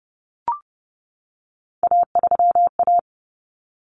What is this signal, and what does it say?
Morse code, heard as a keyed tone, about 30 wpm, 700 Hz. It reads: A3A
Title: Common Cut Numbers - 3-Digit 30wpm